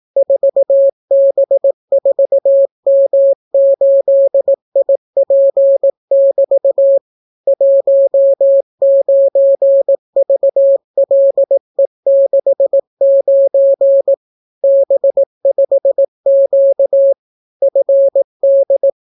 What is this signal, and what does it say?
Morse code, heard as a keyed tone, about 18 wpm, 550 Hz. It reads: 4B4M8IP<BT> 19VLE69 B5Q FD